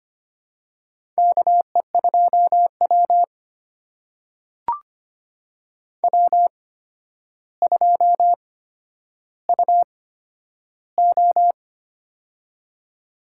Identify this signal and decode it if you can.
Morse code, heard as a keyed tone, about 25 wpm, spaced as for 11 wpm, 700 Hz. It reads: KE2W W2UO
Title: USA Call Signs - 4 Characters - 8x Character Spacing - 25wpm